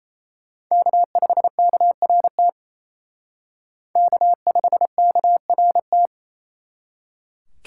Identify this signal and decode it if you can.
Morse code, heard as a keyed tone, about 36 wpm, 700 Hz. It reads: K5KRT K5KRT